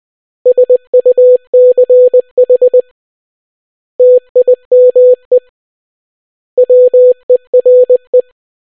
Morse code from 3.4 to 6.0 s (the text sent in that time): TIME